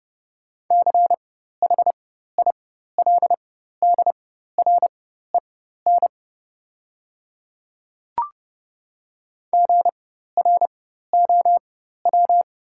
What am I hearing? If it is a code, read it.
Morse code, heard as a keyed tone, about 30 wpm, spaced as for 14 wpm, 690 Hz. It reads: CHILDREN GROW
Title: 4x Character Spacing - Top 500 Words - 30wpm